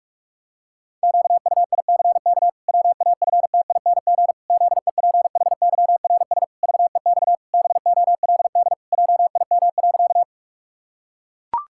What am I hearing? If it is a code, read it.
Morse code, heard as a keyed tone, about 41 wpm, 700 Hz. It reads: QUICK WAFTING ZEPHYRS VEX BOLD JIM.
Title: Pangram Challenge v2 - 45wpm